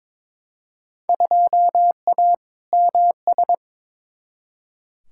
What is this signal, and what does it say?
Morse code, heard as a keyed tone, about 21 wpm, 700 Hz. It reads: 2A MS
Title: Field Day Exchanges - 22wpm